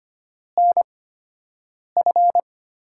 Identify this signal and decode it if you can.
Morse code, heard as a keyed tone, about 25 wpm, spaced as for 7 wpm, 700 Hz. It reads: NF
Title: USA Call Signs - 6 Characters - 8x Character Spacing - 25wpm